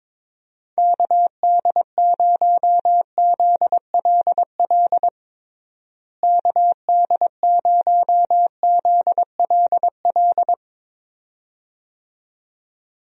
Morse code, heard as a keyed tone, about 22 wpm, 700 Hz. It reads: KD0ZLL KD0ZLL
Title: USA Call Signs - 6 Characters - Repeated 2x - 22wpm